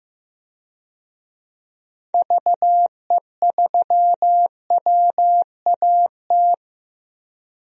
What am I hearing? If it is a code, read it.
Morse code, heard as a keyed tone, about 15 wpm, 700 Hz. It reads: VE3WAT
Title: DXCC-100 Call Signs - 6 Characters - 15wpm